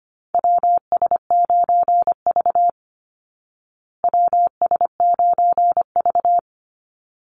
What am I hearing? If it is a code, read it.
Morse code, heard as a keyed tone, about 25 wpm, 700 Hz. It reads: WS9V WS9V
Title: Call Signs Repeated 3x - 25wpm